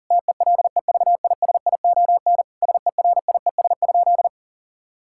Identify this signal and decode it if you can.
Morse code, heard as a keyed tone, about 37 wpm, 700 Hz. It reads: TELEVISION SERIES?
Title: Sentences from Top 1000 Words - Review Part 5 - 40wpm